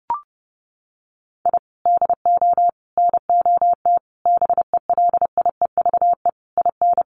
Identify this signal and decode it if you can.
Morse code, heard as a keyed tone, about 30 wpm, 700 Hz. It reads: I DO NOT BELIEVE IN
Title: Sets of 5 Words Encore 30wpm